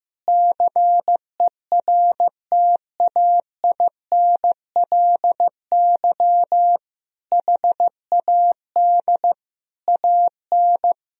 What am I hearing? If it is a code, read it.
Morse code, heard as a keyed tone, about 15 wpm, 700 Hz. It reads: CERTAINLY HAD AN
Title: Sentences from Top 1000 Words - Review Part 4 - 15wpm